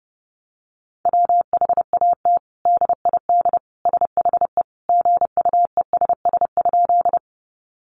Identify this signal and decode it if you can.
Morse code, heard as a keyed tone, about 30 wpm, 700 Hz. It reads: WHAT DID SHE GUESS?
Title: Sentences from Top 1500 Words 30wpm